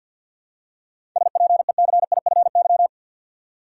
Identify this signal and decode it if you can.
Morse code, heard as a keyed tone, about 49 wpm, 700 Hz. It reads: SPECIFY
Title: Seven Letter Words - 50wpm